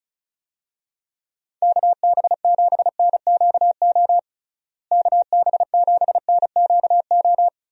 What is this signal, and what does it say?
Morse code, heard as a keyed tone, about 36 wpm, 700 Hz. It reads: KB7NQO KB7NQO
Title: USA Call Signs - 6 Characters - Repeated 3x - 35wpm